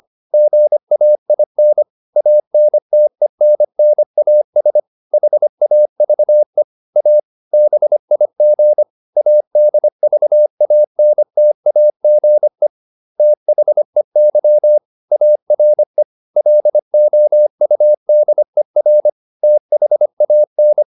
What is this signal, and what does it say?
Morse code, heard as a keyed tone, about 25 wpm, 600 Hz. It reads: GAIN ANTENNAS HAVE A BIG ADVANTAGE THEY ARE LOUDER THAN
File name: SS305 25